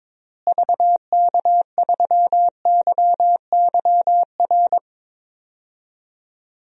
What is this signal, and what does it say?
Morse code, heard as a keyed tone, about 22 wpm, 700 Hz. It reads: VK3YYR
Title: DXCC-100 Call Signs - 6 Characters - Repeated 2x - 22wpm